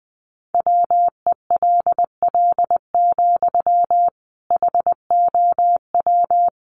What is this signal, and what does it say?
Morse code, heard as a keyed tone, about 20 wpm, 700 Hz. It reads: WELL, HOW
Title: Sentences from Top 200 Words 20wpm